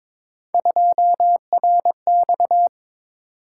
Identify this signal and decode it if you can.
Morse code, heard as a keyed tone, about 22 wpm, 700 Hz. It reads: 2RX
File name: practice / 22wpm / Three Letter-Number Combinations - 22wpm